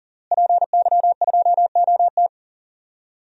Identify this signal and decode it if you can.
Morse code, heard as a keyed tone, about 40 wpm, 700 Hz. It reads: PY2OT
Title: DXCC-100 Call Signs - 5 Characters - Repeated 3x - 40wpm